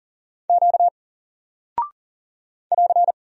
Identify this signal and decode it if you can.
Morse code, heard as a keyed tone, about 40 wpm, 700 Hz. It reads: Q <AR>
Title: Single Character (Letter:Number:Punctuation:Prosign) Alt 40wpm